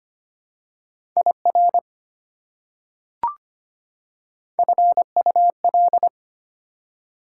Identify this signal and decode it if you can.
Morse code, heard as a keyed tone, about 25 wpm, 700 Hz. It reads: IR FUL
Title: Prefixes and Suffixes 25wpm